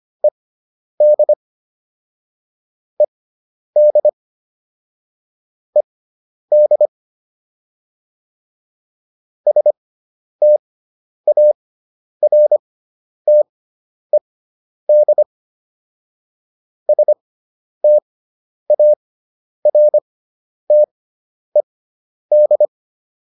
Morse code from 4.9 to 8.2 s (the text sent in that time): ED